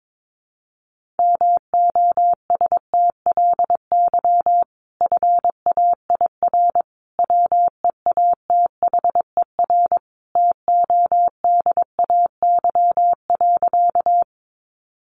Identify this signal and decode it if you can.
Morse code, heard as a keyed tone, about 22 wpm, 700 Hz. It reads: MOSTLY FAIR WEATHER TODAY.
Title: Sentences from Top 1300 Words 22wpm